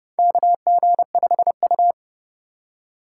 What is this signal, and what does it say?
Morse code, heard as a keyed tone, about 30 wpm, 700 Hz. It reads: KG5U
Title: USA Call Signs - 3x Character Spacing - 30wpm